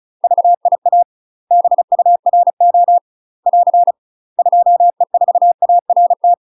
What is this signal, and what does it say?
Morse code, heard as a keyed tone, about 34 wpm, 700 Hz. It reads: VIA BURO <AR> 2E4ART
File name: UK MASTER QSOs TRAINING / 35WPM / UK MASTER QSOs 0038